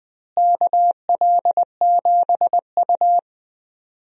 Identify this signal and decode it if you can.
Morse code, heard as a keyed tone, about 21 wpm, 700 Hz. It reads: KL7U
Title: DXCC-100 Call Signs - 4 Characters - Repeated 2x - 20wpm